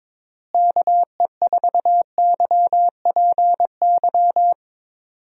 Morse code, heard as a keyed tone, about 22 wpm, 700 Hz. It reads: KE4YPY
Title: USA Call Signs - 6 Characters - Repeated 3x - 22wpm